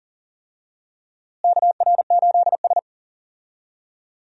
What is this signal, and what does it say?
Morse code, heard as a keyed tone, about 37 wpm, 700 Hz. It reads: KR8S
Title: USA Call Signs - 4 Characters - Repeated 3x - 40wpm